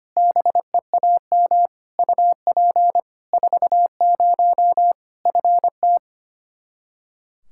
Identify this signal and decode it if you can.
Morse code, heard as a keyed tone, about 25 wpm, 700 Hz. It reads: BEAM UP 40 FT
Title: Practice QSO - Exchange 2 - Info - 25wpm